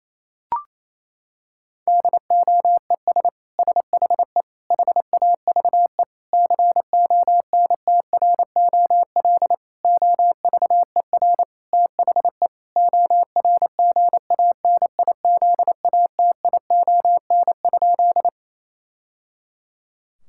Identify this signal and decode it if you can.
Morse code, heard as a keyed tone, about 28 wpm, 700 Hz. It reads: DOES SHE HAVE CONTROL OVER THE ORGANIZATION?
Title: Sentences from Top 1000 Words - Review Part 5 - 28wpm